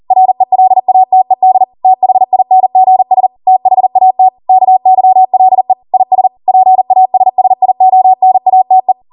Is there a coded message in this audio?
Yes: Morse code from 0.1 to 8.9 s, heard as a keyed tone, about 40 wpm, 750 Hz. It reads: RELATED THINGS THAT KYLE IS PASSIONATE